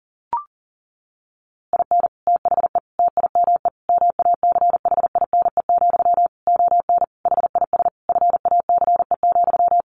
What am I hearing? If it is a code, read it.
Morse code, heard as a keyed tone, about 37 wpm, 700 Hz. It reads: IN THE TIME MACHINE, ON HIS FACE,
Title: A Sound Of Thunder 40wpm